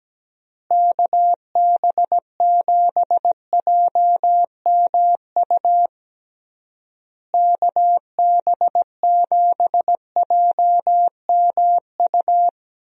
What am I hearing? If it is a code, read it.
Morse code, heard as a keyed tone, about 17 wpm, 700 Hz. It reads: KB7JMU KB7JMU
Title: USA Call Signs - 6 Characters - Repeated 3x - 17wpm